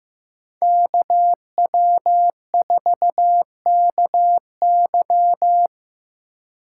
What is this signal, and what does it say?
Morse code, heard as a keyed tone, about 19 wpm, 700 Hz. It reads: KW4KY